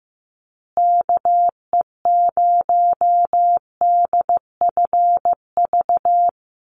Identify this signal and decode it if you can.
Morse code, heard as a keyed tone, about 15 wpm, 700 Hz. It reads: KE0DFV
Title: USA Call Signs Repeated 3x 15wpm